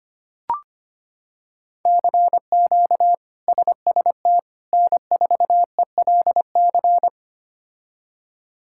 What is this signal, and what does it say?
Morse code, heard as a keyed tone, about 25 wpm, 700 Hz. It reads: CQ SST N4ELC
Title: K1USN SST QSO Practice - 25wpm